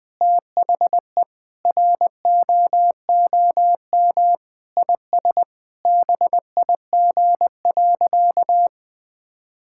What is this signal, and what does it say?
Morse code, heard as a keyed tone, about 20 wpm, 700 Hz. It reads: THE ROOM IS BIG.
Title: Sentences with 3-Word Phrases - 20wpm